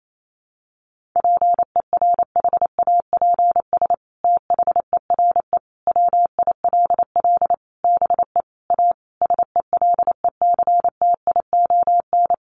Morse code, heard as a keyed tone, about 28 wpm, 700 Hz. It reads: PERHAPS THERE WILL BE A SELECTION
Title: Sentences from Top 1400 Words 28wpm